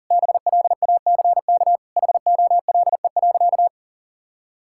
Morse code, heard as a keyed tone, about 40 wpm, 700 Hz. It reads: BLACK HOLE.
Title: Sentences from Top 1600 Words 40wpm